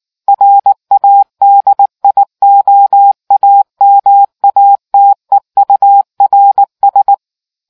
Morse code, heard as a keyed tone, about 19 wpm, 800 Hz. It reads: RADIOAMATEURS